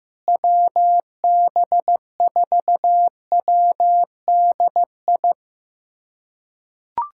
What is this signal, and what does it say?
Morse code, heard as a keyed tone, about 15 wpm, 700 Hz. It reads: WB4WDI
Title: USA Callsign DE Callsign - 6 Characters - 15wpm